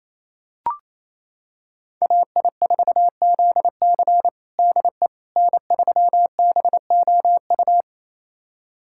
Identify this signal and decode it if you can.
Morse code, heard as a keyed tone, about 28 wpm, 700 Hz. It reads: AI4ZC DE N3BOU
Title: USA Callsign DE Callsign - 5 Characters - 28wpm